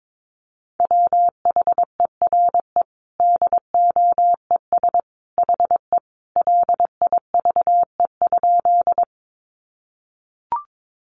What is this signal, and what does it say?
Morse code, heard as a keyed tone, about 22 wpm, 700 Hz. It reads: WHERE DOES HE LIVE?
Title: Sentences from Top 200 Words 22wpm